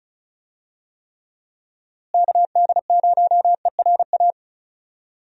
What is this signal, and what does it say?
Morse code, heard as a keyed tone, about 34 wpm, 700 Hz. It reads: KD0ERA
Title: USA Call Signs - 6 Characters - Repeated 2x - 35wpm